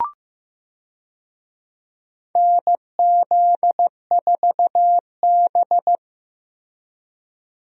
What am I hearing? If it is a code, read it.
Morse code, heard as a keyed tone, about 16 wpm, 700 Hz. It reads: NZ4B
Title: USA Call Signs - 4 Characters - Repeated 3x - 15wpm